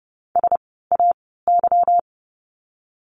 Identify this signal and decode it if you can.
Morse code, heard as a keyed tone, about 30 wpm, spaced as for 17 wpm, 700 Hz. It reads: SAY
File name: course / 30:10wpm / 230 Lesson 37f - Letter x - Words 30wpm